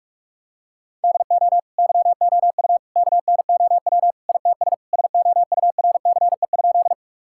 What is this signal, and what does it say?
Morse code, heard as a keyed tone, about 46 wpm, 700 Hz. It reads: DO YOU KNOW ITS SOURCE?
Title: Sentences from Top 1000 Words - Review Part 5 - 45wpm